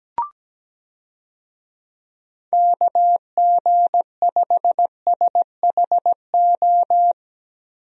Morse code, heard as a keyed tone, about 17 wpm, 700 Hz. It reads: KG5SHO
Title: USA Call Signs - 6 Characters - Repeated 2x - 17wpm